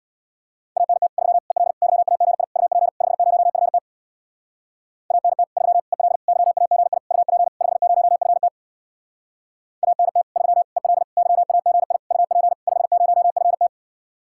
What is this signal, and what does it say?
Morse code, heard as a keyed tone, about 67 wpm, 700 Hz. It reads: ANT 3 EL YAGI UP 50FT ANT 3 EL YAGI UP 50FT ANT 3 EL YAGI UP 50FT